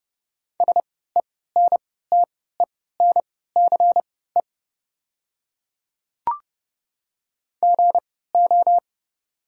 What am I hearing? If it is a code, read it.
Morse code, heard as a keyed tone, about 30 wpm, spaced as for 17 wpm, 700 Hz. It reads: SENTENCE GO
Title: 3x Character Spacing - Top 200 Words - 30wpm